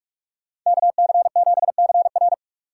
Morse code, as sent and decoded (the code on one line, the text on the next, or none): -.- -.-. --... -.-. .-.
KC7CR